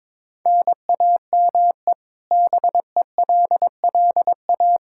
Morse code, heard as a keyed tone, about 22 wpm, 700 Hz. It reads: NAME BELLA
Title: Practice QSO - Exchange 1 - Beginning - 22wpm